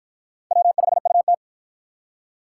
Morse code, heard as a keyed tone, about 53 wpm, 700 Hz. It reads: W5UT